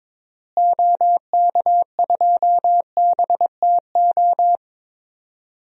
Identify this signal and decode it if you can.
Morse code, heard as a keyed tone, about 21 wpm, 700 Hz. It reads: OK2BTO
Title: DXCC-100 Call Signs - 6 Characters - Repeated 2x - 22wpm